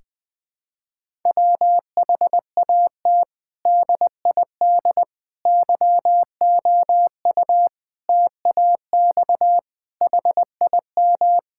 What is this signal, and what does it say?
Morse code, heard as a keyed tone, about 20 wpm, 700 Hz. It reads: WHAT DID YOU TAX HIM
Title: Sentences from Top 1000 Words - Review Part 5 - 20wpm